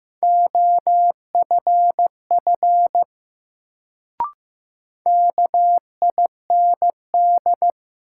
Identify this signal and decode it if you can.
Morse code, heard as a keyed tone, about 15 wpm, 700 Hz. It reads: OFF KIND
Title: Sentences from Top 200 Words - New Word Drill - 15wpm